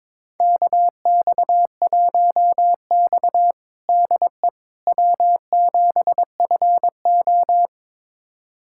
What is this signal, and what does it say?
Morse code, heard as a keyed tone, about 22 wpm, 700 Hz. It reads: KX1X DE W7FO
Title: USA Callsign DE Callsign - 4 Characters - 22wpm